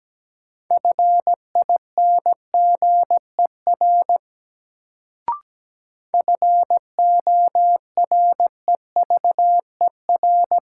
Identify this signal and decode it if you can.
Morse code, heard as a keyed tone, about 17 wpm, 700 Hz. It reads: FINGER FOREVER
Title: Sentences from Top 1600 Words - New Word Drill - 17wpm